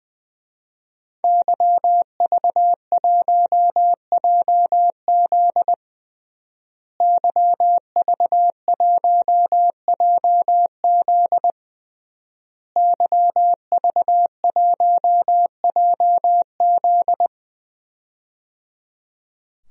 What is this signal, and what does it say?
Morse code, heard as a keyed tone, about 21 wpm, 700 Hz. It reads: YV1JZ YV1JZ YV1JZ
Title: DXCC-100 Call Signs - 5 Characters - Repeated 3x - 20wpm